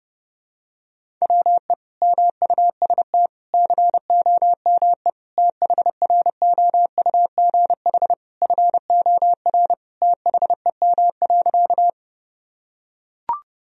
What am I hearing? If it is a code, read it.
Morse code, heard as a keyed tone, about 29 wpm, 700 Hz. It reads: WE MUST COME THROUGH FOR THEM.
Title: Sentences from Top 1000 Words - Review Part 1 - 30wpm